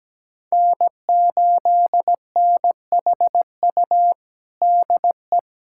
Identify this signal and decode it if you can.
Morse code, heard as a keyed tone, about 17 wpm, 700 Hz. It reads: N8NHU DE
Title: USA Callsign DE Callsign - 5 Characters - 17wpm